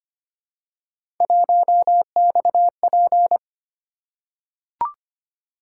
Morse code, heard as a keyed tone, about 25 wpm, 700 Hz. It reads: 1XP
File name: Three Letter-Number Combinations - 25wpm